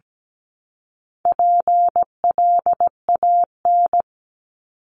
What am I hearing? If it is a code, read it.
Morse code, heard as a keyed tone, about 17 wpm, 700 Hz. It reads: PLAN
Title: ING Suffix - Top 100 Words with Base Words 17wpm